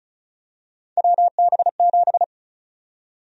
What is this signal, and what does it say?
Morse code, heard as a keyed tone, about 34 wpm, 700 Hz. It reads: WB7
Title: USA Call Sign Prefixes - Repeated 2x - 35wpm